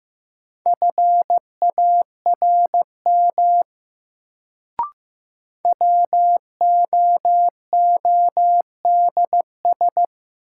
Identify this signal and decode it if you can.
Morse code, heard as a keyed tone, about 15 wpm, 700 Hz. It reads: FARM WOODS
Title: Sentences from Top 1000 Words - Review Part 2 - New Word Drill - 15wpm